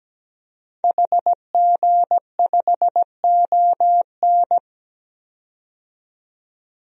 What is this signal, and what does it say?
Morse code, heard as a keyed tone, about 17 wpm, 700 Hz. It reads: HG5ON
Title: DXCC-100 Call Signs - 5 Characters - 17wpm